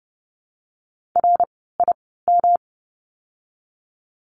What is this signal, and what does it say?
Morse code, heard as a keyed tone, about 30 wpm, spaced as for 17 wpm, 700 Hz. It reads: RIM
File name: course / 30:10wpm / 108 Lesson 18f - Letter m - Words 30wpm